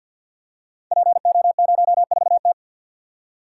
Morse code, heard as a keyed tone, about 52 wpm, 700 Hz. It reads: PO04T
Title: Code Groups - 5 Letter:Numbers - 50wpm